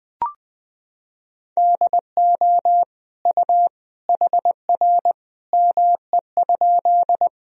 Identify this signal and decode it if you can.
Morse code, heard as a keyed tone, about 21 wpm, 700 Hz. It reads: DO U HR ME?
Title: Sentences from Top 300 Words and Abbreviations 20wpm